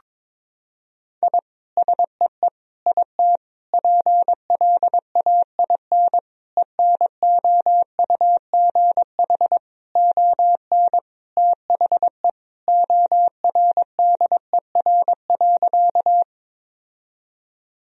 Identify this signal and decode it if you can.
Morse code, heard as a keyed tone, about 22 wpm, 700 Hz. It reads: I SEE IT PLAIN ENOUGH ON THE ORDER.
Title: Sentences from Top 1000 Words - Review Part 2 - 22wpm